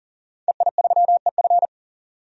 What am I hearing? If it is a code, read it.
Morse code, heard as a keyed tone, about 40 wpm, 700 Hz. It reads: EI3EF
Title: DXCC-100 Call Signs - 5 Characters - 40wpm